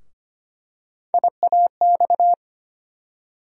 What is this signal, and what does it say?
Morse code, heard as a keyed tone, about 25 wpm, 700 Hz. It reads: IAX
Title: USA Call Sign Postfixes - 3 Letters 25wpm